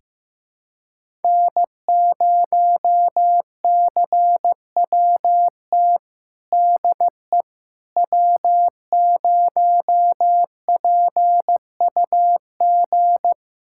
Morse code, heard as a keyed tone, about 16 wpm, 700 Hz. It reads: N0CWT DE W0PUG
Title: USA Callsign DE Callsign - 5 Characters - 15wpm